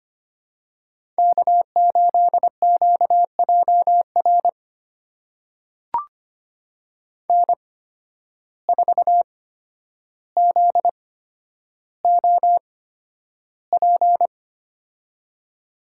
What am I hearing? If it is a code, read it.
Morse code, heard as a keyed tone, about 25 wpm, spaced as for 11 wpm, 700 Hz. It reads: K8QJR N4ZOP
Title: USA Call Signs - 8x Character Spacing - 25wpm